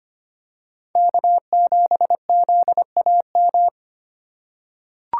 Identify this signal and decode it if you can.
Morse code, heard as a keyed tone, about 25 wpm, 700 Hz. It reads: K7ZAM
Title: USA Call Signs - 5 Characters - 4x Character Spacing - 25wpm